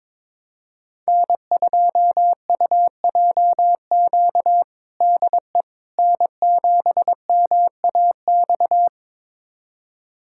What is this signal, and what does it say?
Morse code, heard as a keyed tone, about 22 wpm, 700 Hz. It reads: N2UJQ DE N7MAX
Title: USA Callsign DE Callsign - 5 Characters - 22wpm